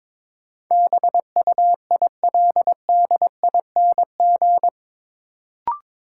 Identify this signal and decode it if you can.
Morse code, heard as a keyed tone, about 21 wpm, 700 Hz. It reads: BUILDING
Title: Sentences from Top 700 Words - New Word Drill - 22wpm